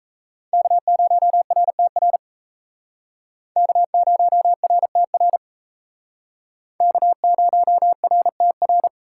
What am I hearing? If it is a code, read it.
Morse code, heard as a keyed tone, about 34 wpm, 700 Hz. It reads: K0RTR K0RTR K0RTR
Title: Speed Racing - USA 1x3 Call Signs 28wpm